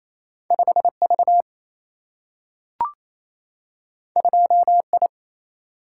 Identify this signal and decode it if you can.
Morse code, heard as a keyed tone, about 28 wpm, 700 Hz. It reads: 5V 2I
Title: Single-Letter Single-Number 28wpm